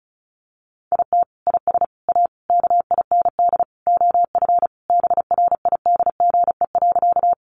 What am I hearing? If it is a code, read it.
Morse code, heard as a keyed tone, about 34 wpm, 700 Hz. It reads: IT IS A KIND OF BRIDGE.
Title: Sentences from Top 1200 Words 35wpm